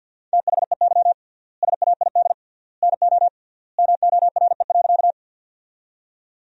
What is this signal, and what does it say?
Morse code, heard as a keyed tone, about 50 wpm, 700 Hz. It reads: THEY SAID NO MORE.
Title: 3x Word-Spacing - Sentences from Top 100 Words - 50wpm